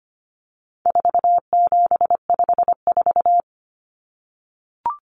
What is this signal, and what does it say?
Morse code, heard as a keyed tone, about 25 wpm, 700 Hz. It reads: 4754
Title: Common Cut Numbers - 4-Digit 25wpm